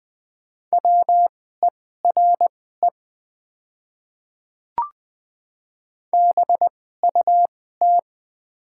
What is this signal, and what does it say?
Morse code, heard as a keyed tone, about 20 wpm, spaced as for 14 wpm, 700 Hz. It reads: WERE BUT
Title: 2x Character Spacing - Top 100 Words - 20wpm